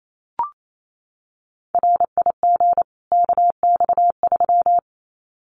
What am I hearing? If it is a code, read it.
Morse code, heard as a keyed tone, about 28 wpm, 700 Hz. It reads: RIG KX3
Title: Common QSO Phrases 28wpm